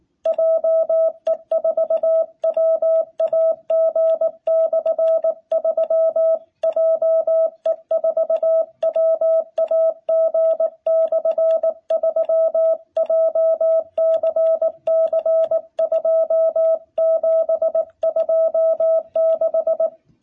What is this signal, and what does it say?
Morse code, heard as a keyed tone, about 18 wpm, 660 Hz. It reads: JE4WAG/3 JE4WAG/3 JCC2726
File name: jh8cbh / 2021cw